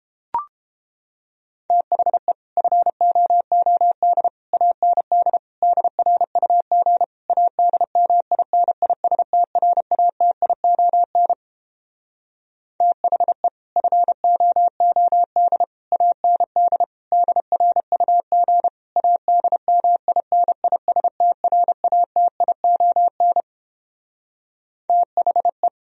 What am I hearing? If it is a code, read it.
Morse code, heard as a keyed tone, about 29 wpm, 700 Hz. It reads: THE FOOD AND DRUG ADMINISTRATION THE FOOD AND DRUG ADMINISTRATION THE